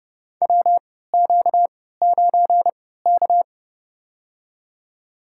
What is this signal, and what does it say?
Morse code, heard as a keyed tone, about 31 wpm, spaced as for 17 wpm, 700 Hz. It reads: WQ9K
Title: USA Call Signs - 4 Characters - 3x Character Spacing - 30wpm